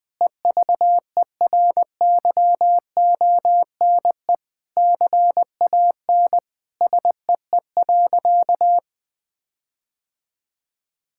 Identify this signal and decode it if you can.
Morse code, heard as a keyed tone, about 20 wpm, 700 Hz. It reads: EVERYONE CAN SEE.
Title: Sentences with 5-Word Phrases - 20wpm